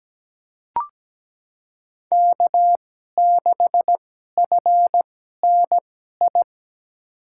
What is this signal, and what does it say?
Morse code, heard as a keyed tone, about 17 wpm, spaced as for 12 wpm, 700 Hz. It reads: K6FNI
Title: USA Call Signs - 5 Characters - 2x Character Spacing - 17wpm